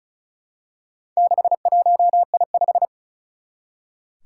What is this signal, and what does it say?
Morse code, heard as a keyed tone, about 36 wpm, 700 Hz. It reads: 61I5
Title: Four Challenging Letter-Number Combinations 35wpm